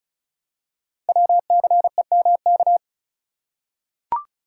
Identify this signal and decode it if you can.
Morse code, heard as a keyed tone, about 36 wpm, 700 Hz. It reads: WCEMK
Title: Code Groups - 5 Letters - 35wpm